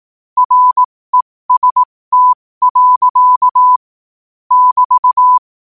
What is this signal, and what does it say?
Morse code, heard as a keyed tone, about 18 wpm, spaced as for 15 wpm, 1000 Hz. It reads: REST. <BT>
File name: qotdfiles / 15wpm